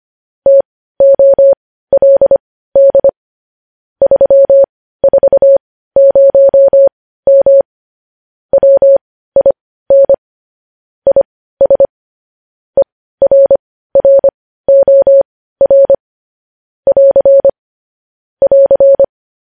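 Morse code, heard as a keyed tone, about 25 wpm, spaced as for 15 wpm, 550 Hz. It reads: TOLD 340M WIN IS ERROR <AR> <AR>
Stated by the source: Morse code practice of news headlines at 15 wpm.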